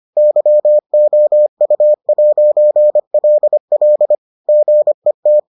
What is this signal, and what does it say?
Morse code, heard as a keyed tone, about 25 wpm, 600 Hz. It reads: YOU'LL GET